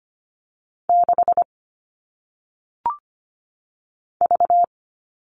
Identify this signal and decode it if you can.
Morse code, heard as a keyed tone, about 25 wpm, 700 Hz. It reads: 6 V
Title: All Cut Numbers - 1-Digit 25wpm